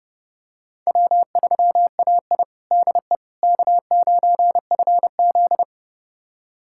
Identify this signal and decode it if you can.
Morse code, heard as a keyed tone, about 32 wpm, 700 Hz. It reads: W3AI DE K9FZ
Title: USA Callsign DE Callsign - 4 Characters - 30wpm